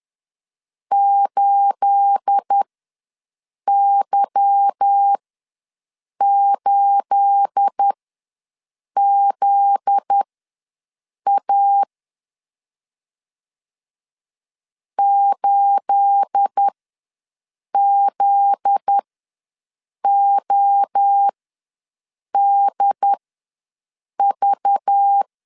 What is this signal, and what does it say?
Morse code, heard as a keyed tone, about 10 wpm, spaced as for 6 wpm, 800 Hz. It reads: 8Y8ZA 8ZODV
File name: LEZIONI / 9 28 ws 10wpm 4124